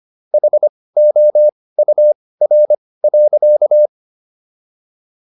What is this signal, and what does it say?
Morse code, heard as a keyed tone, about 25 wpm, spaced as for 18 wpm, 600 Hz. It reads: HOUR.